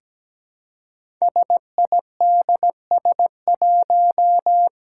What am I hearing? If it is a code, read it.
Morse code, heard as a keyed tone, about 17 wpm, 700 Hz. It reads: SIDS1
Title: Five Challenging Letter-Number Combinations 17wpm